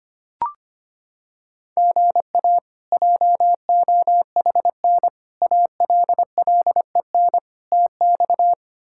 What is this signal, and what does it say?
Morse code, heard as a keyed tone, about 25 wpm, 700 Hz. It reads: GA JOHN ALLEN TX
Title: K1USN SST QSO Practice - 25wpm